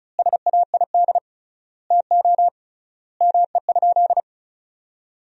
Morse code, heard as a keyed tone, about 36 wpm, 700 Hz. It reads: SAID TO ME?
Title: 3x Word-Spacing - Sentences from Top 200 Words - 35wpm